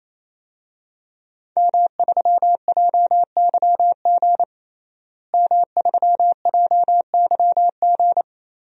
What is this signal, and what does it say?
Morse code, heard as a keyed tone, about 27 wpm, 700 Hz. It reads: M3JYG M3JYG
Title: DXCC-100 Call Signs - 5 Characters - Repeated 3x - 28wpm